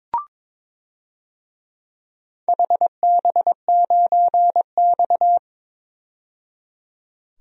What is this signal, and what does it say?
Morse code, heard as a keyed tone, about 21 wpm, 700 Hz. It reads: HB9X
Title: DXCC-100 Call Signs - 4 Characters - 22wpm